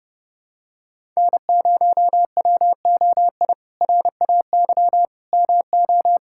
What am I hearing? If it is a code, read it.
Morse code, heard as a keyed tone, about 32 wpm, 700 Hz. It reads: N0WOI RAY MO
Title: K1USN SST QSO Practice - 30wpm